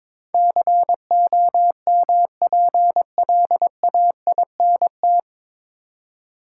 Morse code, heard as a keyed tone, about 22 wpm, 700 Hz. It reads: COMPLAINT
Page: Morse Code Ninja practice 22wpm